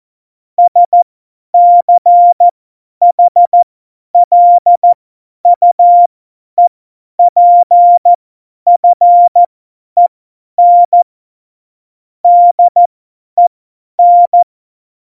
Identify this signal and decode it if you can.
Morse code, heard as a keyed tone, about 14 wpm, spaced as for 10 wpm, 700 Hz. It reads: SCHLUEPFEN DEN
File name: Morsekurs / Witze / Witz 07. November 2017, 50 F70